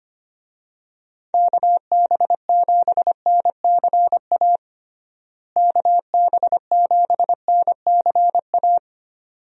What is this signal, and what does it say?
Morse code, heard as a keyed tone, about 25 wpm, 700 Hz. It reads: KB7NCA KB7NCA
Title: USA Call Signs - 6 Characters - Repeated 2x - 25wpm